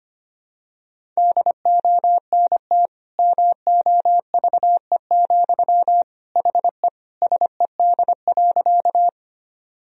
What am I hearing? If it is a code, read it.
Morse code, heard as a keyed tone, about 25 wpm, 700 Hz. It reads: DONT MOVE, HE SED.
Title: Sentences from Top 300 Words and Abbreviations 25wpm